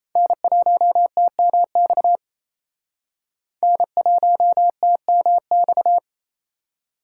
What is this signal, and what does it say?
Morse code, heard as a keyed tone, about 35 wpm, 700 Hz. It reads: N1TMX N1TMX
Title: Speed Racing - USA 1x3 Call Signs 28wpm